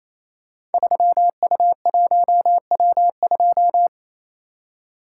Morse code, heard as a keyed tone, about 28 wpm, 700 Hz. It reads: 3U1W2
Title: Code Groups - 5 Letter:Numbers - 28wpm